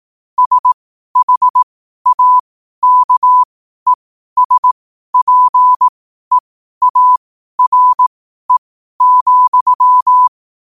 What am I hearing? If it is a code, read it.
Morse code, heard as a keyed tone, about 18 wpm, spaced as for 12 wpm, 1000 Hz. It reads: SHAKESPEARE,
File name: qotdfiles / 12wpm